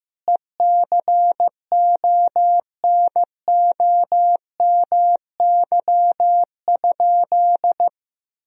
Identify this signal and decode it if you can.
Morse code, heard as a keyed tone, about 15 wpm, 700 Hz. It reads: ECONOMY?
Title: Sentences from Top 1000 Words - Review Part 4 - 15wpm